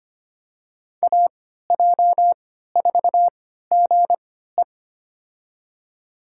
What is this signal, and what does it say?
Morse code, heard as a keyed tone, about 25 wpm, spaced as for 14 wpm, 700 Hz. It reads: AJ4GE
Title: USA Call Signs - 5 Characters - 3x Character Spacing - 25wpm